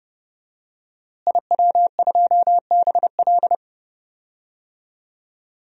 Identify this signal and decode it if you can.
Morse code, heard as a keyed tone, about 29 wpm, 700 Hz. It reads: IW2BL
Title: DXCC-100 Call Signs - 5 Characters - Repeated 3x - 30wpm